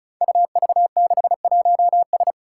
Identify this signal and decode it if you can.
Morse code, heard as a keyed tone, about 36 wpm, 700 Hz. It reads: UV61S
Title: Five Challenging Letter-Number Combinations 35wpm